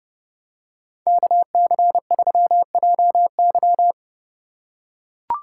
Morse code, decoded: KC3JY

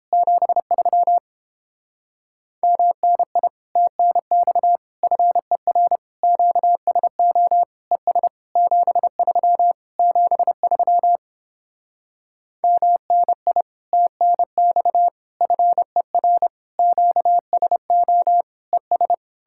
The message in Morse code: 73 MNI TNX FER QSO ES 73 73 MNI TNX FER QSO ES

